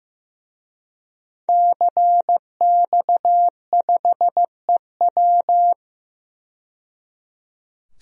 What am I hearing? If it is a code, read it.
Morse code, heard as a keyed tone, about 15 wpm, 700 Hz. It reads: CX5EW
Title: DXCC-100 Call Signs - 5 Characters - 15wpm